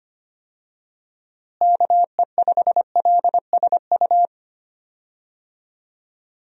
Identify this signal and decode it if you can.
Morse code, heard as a keyed tone, about 25 wpm, 700 Hz. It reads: KE5LSU
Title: USA Call Signs - 6 Characters - Repeated 2x - 25wpm